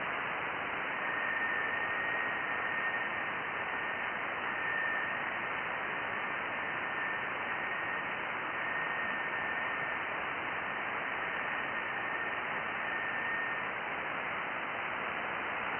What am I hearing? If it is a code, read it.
FT8.
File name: IC-7100